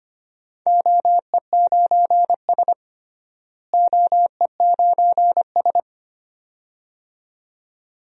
Morse code, heard as a keyed tone, about 25 wpm, 700 Hz. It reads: OE9S OE9S
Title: DXCC-100 Call Signs - 4 Characters - Repeated 2x - 25wpm